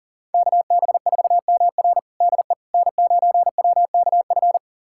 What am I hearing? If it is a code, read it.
Morse code, heard as a keyed tone, about 40 wpm, 700 Hz. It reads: KB4MR DE N9WKF
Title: USA Callsign DE Callsign - 5 Characters - 40wpm